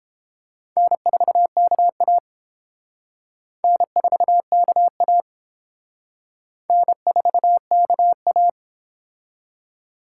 Morse code, heard as a keyed tone, about 30 wpm, 700 Hz. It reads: N4KA N4KA N4KA